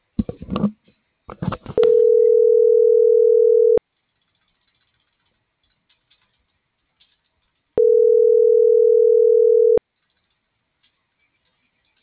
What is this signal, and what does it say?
Ringback tone (North America), 440 + 480 Hz.